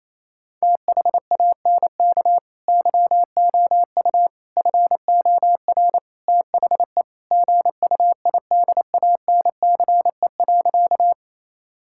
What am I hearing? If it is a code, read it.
Morse code, heard as a keyed tone, about 28 wpm, 700 Hz. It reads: THANK YOU FOR THE GUIDANCE.
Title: Sentences from Top 1600 Words 28wpm